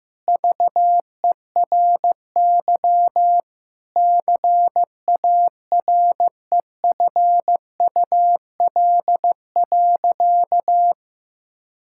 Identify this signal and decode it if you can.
Morse code, heard as a keyed tone, about 15 wpm, 700 Hz. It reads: VERY CAREFUL.
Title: Sentences from Top 1600 Words 15wpm